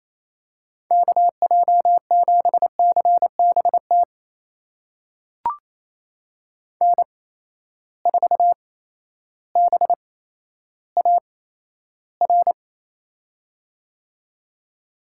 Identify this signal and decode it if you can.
Morse code, heard as a keyed tone, about 28 wpm, spaced as for 13 wpm, 700 Hz. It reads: KJ7CBT N4BAR
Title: USA Call Signs - 8x Character Spacing - 28wpm